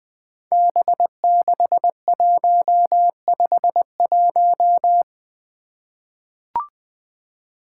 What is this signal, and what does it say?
Morse code, heard as a keyed tone, about 21 wpm, 700 Hz. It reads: B6151
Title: Five Challenging Letter-Number Combinations 20wpm